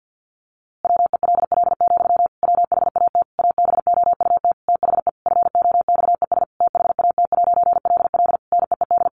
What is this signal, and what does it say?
Morse code, heard as a keyed tone, about 50 wpm, 700 Hz. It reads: WELL, WHAT ABOUT THE FOXES THAT'LL NEED
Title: A Sound Of Thunder 50wpm